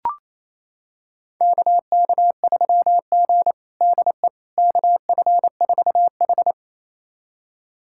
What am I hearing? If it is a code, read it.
Morse code, heard as a keyed tone, about 28 wpm, 700 Hz. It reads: KK3G DE KF4H
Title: USA Callsign DE Callsign - 4 Characters - 28wpm